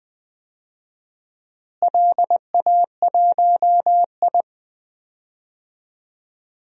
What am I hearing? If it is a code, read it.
Morse code, heard as a keyed tone, about 20 wpm, 700 Hz. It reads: LA1I
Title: DXCC-100 Call Signs - 4 Characters - Repeated 3x - 20wpm